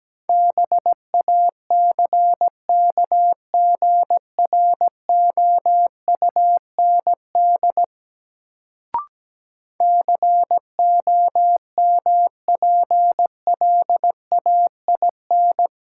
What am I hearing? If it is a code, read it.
Morse code, heard as a keyed tone, about 17 wpm, 700 Hz. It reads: BACKGROUND COMPLAIN
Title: Sentences from Top 1300 Words - New Word Drill Alt - 17wpm